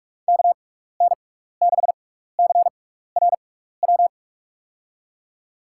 Morse code, heard as a keyed tone, about 45 wpm, spaced as for 15 wpm, 700 Hz. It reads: KN6CRW
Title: USA Call Signs - 6 Characters - 6x Character Spacing - 45wpm